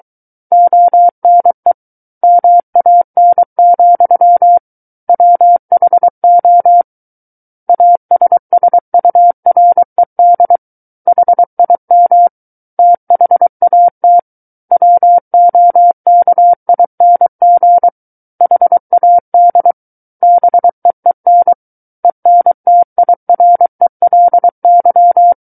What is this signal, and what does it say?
Morse code, heard as a keyed tone, about 23 wpm, 700 Hz. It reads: ONE MAN, WHO ASSURED HIM THAT WOKING HAD BEEN ENTIRELY
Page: SKCC Learning Center - Morse Code Practice Files